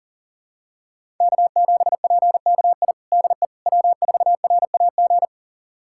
Morse code, heard as a keyed tone, about 37 wpm, 700 Hz. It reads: K7PKI DE W4RAG
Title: USA Callsign DE Callsign - 5 Characters - 40wpm